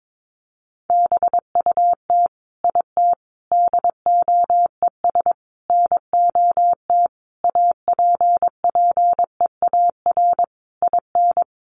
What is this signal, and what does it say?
Morse code, heard as a keyed tone, about 22 wpm, 700 Hz. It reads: BUT IT DOES NOT APPEAR IN
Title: Sentences from Top 1200 Words 22wpm